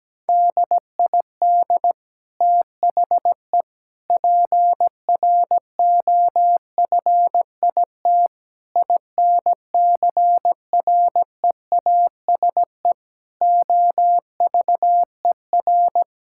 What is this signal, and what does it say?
Morse code, heard as a keyed tone, about 17 wpm, 700 Hz. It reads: DID THE PROFIT INCREASE OVER
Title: Sentences from Top 1000 Words - Review Part 5 - 17wpm